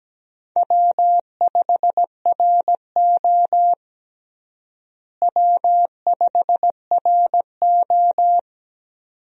Morse code, heard as a keyed tone, about 17 wpm, 700 Hz. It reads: W5RO W5RO
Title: USA Call Signs - 4 Characters - Repeated 2x - 17wpm